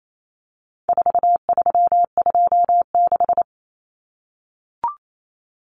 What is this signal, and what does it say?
Morse code, heard as a keyed tone, about 28 wpm, 700 Hz. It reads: 4326
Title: Common Cut Numbers - 4-Digit 28wpm